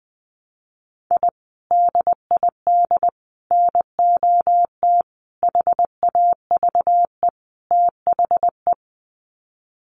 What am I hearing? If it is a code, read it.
Morse code, heard as a keyed tone, about 20 wpm, 700 Hz. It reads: I DID NOT HAVE THE
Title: Sets of 5 Words Encore 20wpm